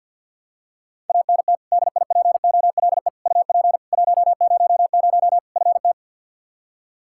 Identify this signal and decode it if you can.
Morse code, heard as a keyed tone, about 52 wpm, 700 Hz. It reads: ANT DIPOLE UP 100 FT
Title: Practice QSO - Exchange 2 - Info - 50wpm